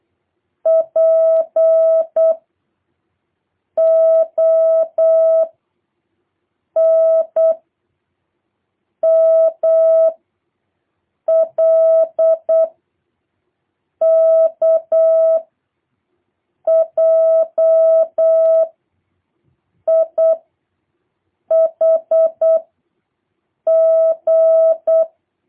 Morse code, heard as a keyed tone, about 8 wpm, 650 Hz. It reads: P O N M L K J I H G